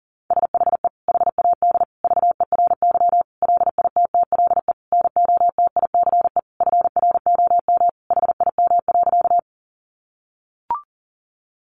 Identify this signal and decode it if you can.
Morse code, heard as a keyed tone, about 40 wpm, 700 Hz. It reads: SHE HAD VERY LITTLE NOTICE FROM HIM.
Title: Sentences from Top 400 Words 40wpm